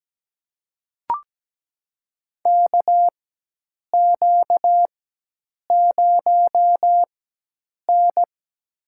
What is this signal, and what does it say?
Morse code, heard as a keyed tone, about 17 wpm, spaced as for 8 wpm, 700 Hz. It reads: KQ0N
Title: USA Call Signs - 4x Character Spacing - 17wpm